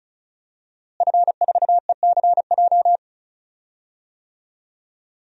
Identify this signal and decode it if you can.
Morse code, heard as a keyed tone, about 34 wpm, 700 Hz. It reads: F4ECJ
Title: DXCC-100 Call Signs - 5 Characters - Repeated 2x - 35wpm